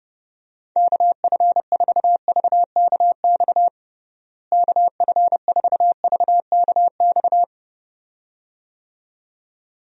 Morse code, heard as a keyed tone, about 30 wpm, 700 Hz. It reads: KF4VKX KF4VKX
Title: USA Call Signs - 6 Characters - Repeated 2x - 30wpm